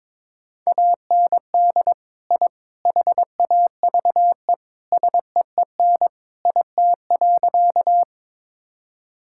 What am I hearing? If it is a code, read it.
Morse code, heard as a keyed tone, about 22 wpm, 700 Hz. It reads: AND I HAVE SEEN IT.
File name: practice / 22wpm / Sentences with 3-Word Phrases - 22wpm